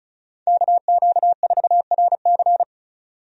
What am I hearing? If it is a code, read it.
Morse code, heard as a keyed tone, about 36 wpm, 700 Hz. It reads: KQ4RC
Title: USA Call Signs - 5 Characters - 35wpm